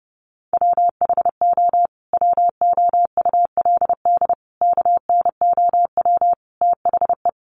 Morse code, heard as a keyed tone, about 30 wpm, 700 Hz. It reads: WHO WOULD KNOW THE
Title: Sentences from Top 800 Words 30wpm